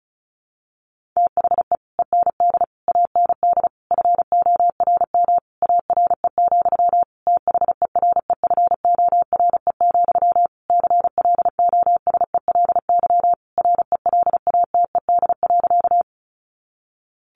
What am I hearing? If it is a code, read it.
Morse code, heard as a keyed tone, about 34 wpm, 700 Hz. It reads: THE END AND FORM ARE, THEREFORE, CLOSELY RELATED.
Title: Sentences from Top 1400 Words 35wpm